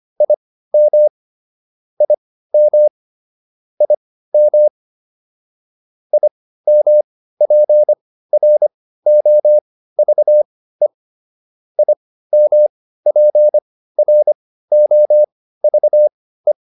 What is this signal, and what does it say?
Morse code, heard as a keyed tone, about 25 wpm, spaced as for 15 wpm, 600 Hz. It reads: IM IM IM IMPROVE IMPROVE